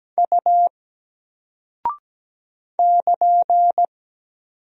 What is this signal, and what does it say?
Morse code, heard as a keyed tone, about 17 wpm, 700 Hz. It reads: U <KN>
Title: Single Character (Letter:Number:Punctuation:Prosign) Alt 17wpm